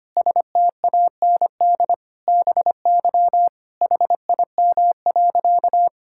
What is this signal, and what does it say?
Morse code, heard as a keyed tone, about 25 wpm, 700 Hz. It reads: STAND BY HIM.
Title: Sentences from Top 1000 Words - Review Part 5 - 25wpm